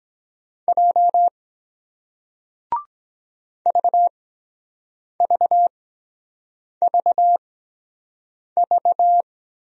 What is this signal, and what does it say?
Morse code, heard as a keyed tone, about 23 wpm, spaced as for 7 wpm, 700 Hz. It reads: J VVVV